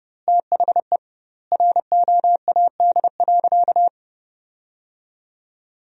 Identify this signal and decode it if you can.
Morse code, heard as a keyed tone, about 30 wpm, 700 Hz. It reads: THE ROAD.
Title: 2x Word-Spacing - Sentences from Top 400 Words - 30wpm